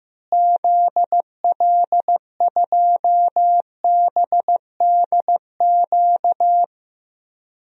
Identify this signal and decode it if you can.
Morse code, heard as a keyed tone, about 15 wpm, 700 Hz. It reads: ZL2BDQ
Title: DXCC-100 Call Signs - 6 Characters - Repeated 2x - 15wpm